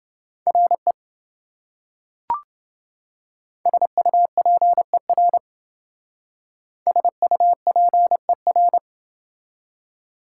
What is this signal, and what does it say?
Morse code, heard as a keyed tone, about 28 wpm, 700 Hz. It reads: RE SUPER SUPER